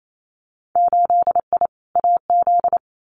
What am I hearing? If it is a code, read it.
Morse code, heard as a keyed tone, about 28 wpm, 700 Hz. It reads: 8I AZ
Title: Winter Field Day Exchanges Encore 28wpm